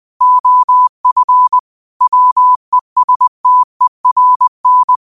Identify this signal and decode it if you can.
Morse code, heard as a keyed tone, about 20 wpm, 1000 Hz. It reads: OF WESTERN